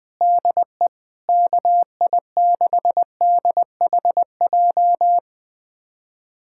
Morse code, heard as a keyed tone, about 20 wpm, 700 Hz. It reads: DE KI6DHJ
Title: USA Callsign DE Callsign 20wpm